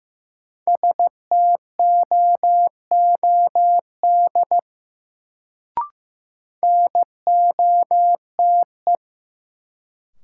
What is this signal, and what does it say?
Morse code, heard as a keyed tone, about 15 wpm, 700 Hz. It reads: STOOD NOTE
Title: Sentences from Top 500 Words - New Word Drill - 15wpm